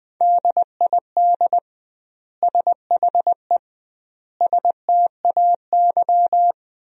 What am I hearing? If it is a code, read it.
Morse code, heard as a keyed tone, about 20 wpm, 700 Hz. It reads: DID SHE STAY
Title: 2x Word-Spacing - Sentences from Top 500 Words - 20wpm